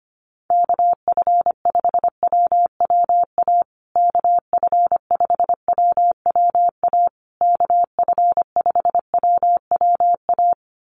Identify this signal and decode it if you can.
Morse code, heard as a keyed tone, about 25 wpm, 700 Hz. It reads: KF5WWA KF5WWA KF5WWA
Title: USA Call Signs Repeated 3x 25wpm